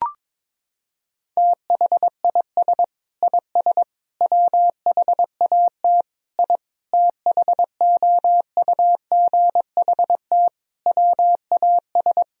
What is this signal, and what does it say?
Morse code, heard as a keyed tone, about 22 wpm, 700 Hz. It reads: THIS IS WHAT I THOUGHT WAS
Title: Sentences from Top 1000 Words - Review Part 4 - 22wpm